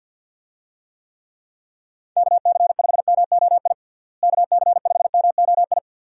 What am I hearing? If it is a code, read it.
Morse code, heard as a keyed tone, about 52 wpm, 700 Hz. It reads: KC5MOI KC5MOI
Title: USA Call Signs - 6 Characters - Repeated 3x - 50wpm